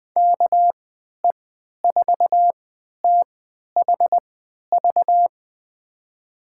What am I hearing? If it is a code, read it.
Morse code, heard as a keyed tone, about 20 wpm, spaced as for 11 wpm, 700 Hz. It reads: KE4THV
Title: USA Call Signs - 6 Characters - 3x Character Spacing - 20wpm